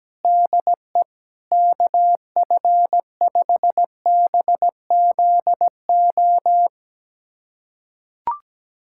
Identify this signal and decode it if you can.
Morse code, heard as a keyed tone, about 17 wpm, 700 Hz. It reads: DE KF5BZO
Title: USA Callsign DE Callsign 17wpm